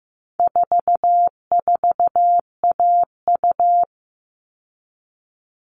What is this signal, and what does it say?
Morse code, heard as a keyed tone, about 15 wpm, 700 Hz. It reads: 44AU
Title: All Cut Numbers - 4-Digit 15wpm